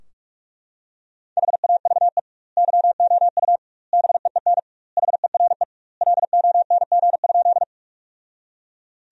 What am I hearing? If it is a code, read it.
Morse code, heard as a keyed tone, about 41 wpm, 700 Hz. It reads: HAVE YOU BEEN HERE LONG?
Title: 2x Word-Spacing - Sentences from Top 200 Words - 45wpm